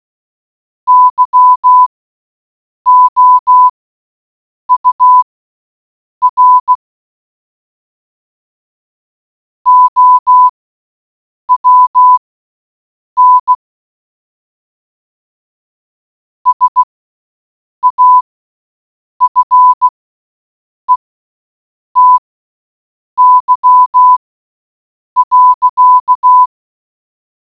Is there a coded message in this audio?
Yes: Morse code from 0.9 to 26.4 s, heard as a keyed tone, about 16 wpm, spaced as for 7 wpm, 1000 Hz. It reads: YOUR OWN SAFETY.